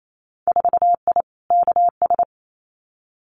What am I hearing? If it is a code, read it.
Morse code, heard as a keyed tone, about 28 wpm, 700 Hz. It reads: 4I KS
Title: Winter Field Day Exchanges Encore 28wpm